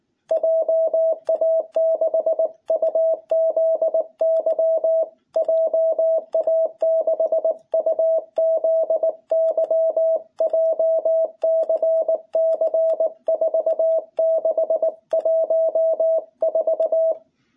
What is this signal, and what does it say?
Morse code, heard as a keyed tone, about 19 wpm, 660 Hz. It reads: JA6UZY JA6UZY JCC 4614
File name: jh8cbh / 2021cw